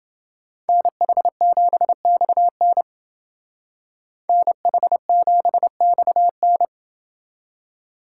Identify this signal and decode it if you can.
Morse code, heard as a keyed tone, about 28 wpm, 700 Hz. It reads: NH7XN NH7XN